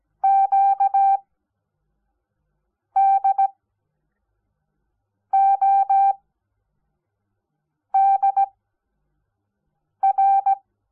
Morse code, heard as a keyed tone, about 17 wpm, spaced as for 4 wpm, 780 Hz. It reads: QDODR
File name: Onlinelehrgang / morsen / DJ4UF Morsen05